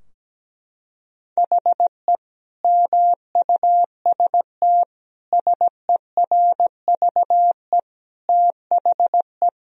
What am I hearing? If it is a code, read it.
Morse code, heard as a keyed tone, about 17 wpm, 700 Hz. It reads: HE MUST SERVE THE
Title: Sentences from Top 1000 Words - Review Part 2 - 17wpm